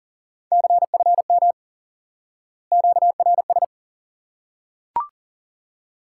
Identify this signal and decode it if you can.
Morse code, heard as a keyed tone, about 40 wpm, 700 Hz. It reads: CFM QRS
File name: IQE Course 3 - Lesson 89 - YAGI - Two Elements - 40wpm